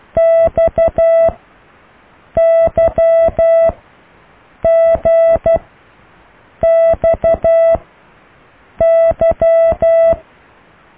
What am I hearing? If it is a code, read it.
Morse code, heard as a keyed tone, about 12 wpm, spaced as for 6 wpm, 660 Hz. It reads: XYGXY